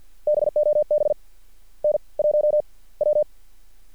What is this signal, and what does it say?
Morse code, heard as a keyed tone, about 50 wpm, spaced as for 30 wpm, 600 Hz. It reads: BOB N1W
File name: archived / CRSnet